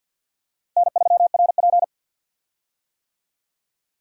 Morse code, heard as a keyed tone, about 52 wpm, 700 Hz. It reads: N3RP
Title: USA Call Signs - 4 Characters - Repeated 2x - 50wpm